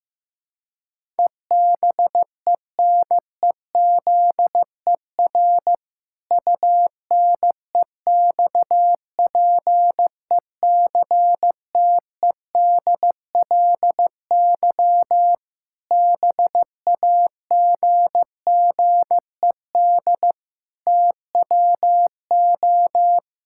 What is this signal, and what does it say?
Morse code, heard as a keyed tone, about 15 wpm, 700 Hz. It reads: EBENEZER UNEXPECTEDLY BAGGED TWO
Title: Pangram Challenge v2 - 15wpm